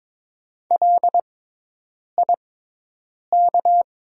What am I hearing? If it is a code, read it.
Morse code, heard as a keyed tone, about 22 wpm, spaced as for 8 wpm, 700 Hz. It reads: LIK
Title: 6x Character Spacing - Top 100 Words - 22wpm